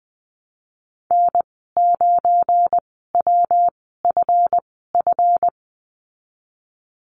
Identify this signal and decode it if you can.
Morse code, heard as a keyed tone, about 20 wpm, spaced as for 15 wpm, 700 Hz. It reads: N9WFF
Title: 186 Lesson 30g - Letter b - Call Signs 20wpm